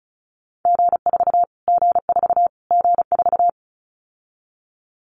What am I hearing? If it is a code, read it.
Morse code, heard as a keyed tone, about 34 wpm, 700 Hz. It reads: G4 G4 G4
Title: Callsign Prefixes for Top 25 DXCC - Repeated 3x - 35wpm